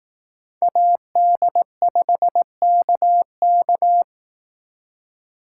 Morse code, heard as a keyed tone, about 18 wpm, 700 Hz. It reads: AD5KK
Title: Speed Racing - USA 2x2 Call Signs 15wpm